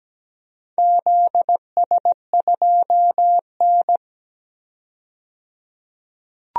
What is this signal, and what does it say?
Morse code, heard as a keyed tone, about 17 wpm, 700 Hz. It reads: ZS2N
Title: DXCC-100 Call Signs - 4 Characters - 17wpm